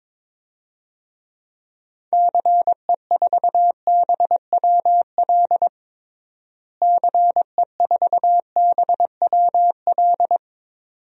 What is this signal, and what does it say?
Morse code, heard as a keyed tone, about 22 wpm, 700 Hz. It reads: CE4BWL CE4BWL
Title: DXCC-100 Call Signs - 6 Characters - Repeated 2x - 22wpm